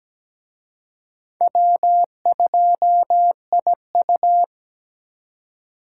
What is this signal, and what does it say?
Morse code, heard as a keyed tone, about 20 wpm, 700 Hz. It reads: W2IU
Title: Speed Racing - USA Call Signs 17wpm